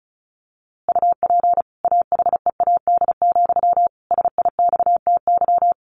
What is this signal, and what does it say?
Morse code, heard as a keyed tone, about 34 wpm, 700 Hz. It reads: UP AHEAD, SIXTY
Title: A Sound Of Thunder 35wpm